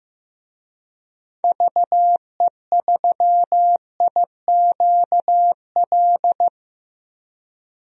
Morse code, heard as a keyed tone, about 15 wpm, 700 Hz. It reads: VE3IQL
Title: DXCC-100 Call Signs - 6 Characters - 15wpm